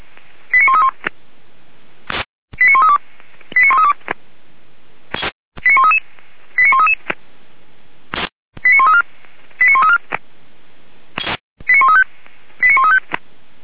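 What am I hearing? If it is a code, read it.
Five-tone selective call (ZVEI), heard as quick runs of five tones, tones of about 70 ms. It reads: 89131 89132 89132 89133 89133 89134 89134 89135 89135
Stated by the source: ZVEI five-tone call received by SDR.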